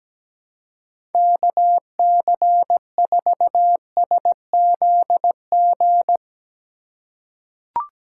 Morse code, heard as a keyed tone, about 17 wpm, 700 Hz. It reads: KC4SZG
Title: USA Call Signs - 6 Characters - 3x Character Spacing - 17wpm